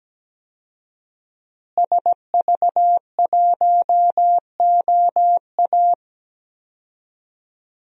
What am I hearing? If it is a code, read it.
Morse code, heard as a keyed tone, about 17 wpm, 700 Hz. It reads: SV1OA
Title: DXCC-100 Call Signs - 5 Characters - Repeated 3x - 17wpm